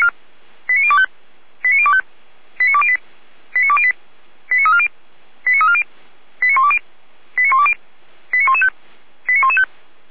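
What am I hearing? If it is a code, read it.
Five-tone selective call (ZVEI), heard as quick runs of five tones, tones of about 70 ms. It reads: 89025 89025 89208 89208 89340 89340 89120 89120 89115 89115
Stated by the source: Five-tone ZVEI selective call 89092, received on an SDR.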